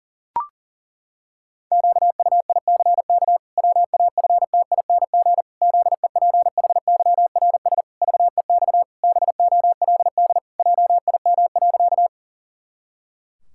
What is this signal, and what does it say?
Morse code, heard as a keyed tone, about 38 wpm, 700 Hz. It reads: QUICK WAFTING ZEPHYRS VEX BOLD JIM.
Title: Pangram Challenge v2 - 40wpm